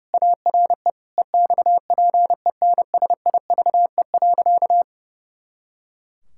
Morse code, heard as a keyed tone, about 30 wpm, 700 Hz. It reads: ARE EXPENSIVE.
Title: Pangram Challenge v2 - 30wpm